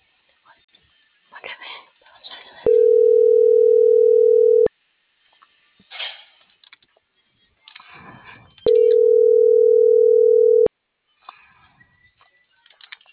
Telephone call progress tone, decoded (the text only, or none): ringback tone